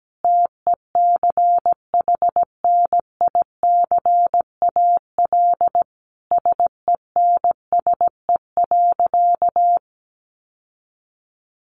Morse code, heard as a keyed tone, about 17 wpm, 700 Hz. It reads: TECHNICAL SENSE.
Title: Sentences from Top 1100 Words 17wpm